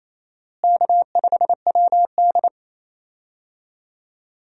Morse code, heard as a keyed tone, about 28 wpm, 700 Hz. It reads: K5WD
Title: USA Call Signs - 4 Characters - Repeated 3x - 28wpm